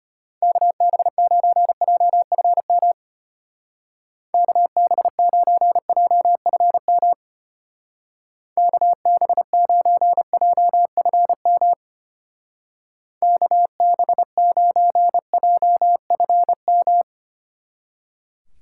Morse code, heard as a keyed tone, about 31 wpm, 700 Hz. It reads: KB9JFM KB9JFM KB9JFM KB9JFM
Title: Speed Racing - USA Call Signs 25wpm